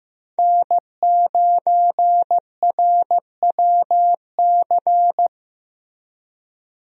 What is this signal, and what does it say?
Morse code, heard as a keyed tone, about 15 wpm, 700 Hz. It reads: N9RWC
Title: USA Call Signs - 5 Characters - Repeated 3x - 15wpm